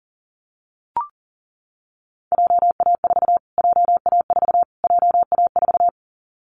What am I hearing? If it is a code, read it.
Morse code, heard as a keyed tone, about 40 wpm, 700 Hz. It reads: JA4 JA4 JA4
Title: Callsign Prefixes for Top 25 DXCC - Repeated 3x - 40wpm